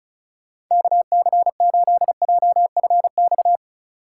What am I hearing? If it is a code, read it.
Morse code, heard as a keyed tone, about 34 wpm, 700 Hz. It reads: KC8JFX
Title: USA Call Signs - 6 Characters - 2x Character Spacing - 35wpm